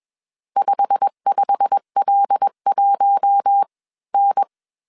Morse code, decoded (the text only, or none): 55L1N